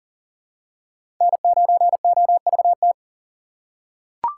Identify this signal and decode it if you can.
Morse code, heard as a keyed tone, about 40 wpm, 700 Hz. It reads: N9OVT
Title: USA Call Signs - 5 Characters - 4x Character Spacing - 40wpm